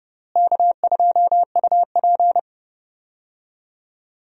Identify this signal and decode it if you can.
Morse code, heard as a keyed tone, about 30 wpm, 700 Hz. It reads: K2UP
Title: USA Call Signs - 4 Characters - Repeated 3x - 30wpm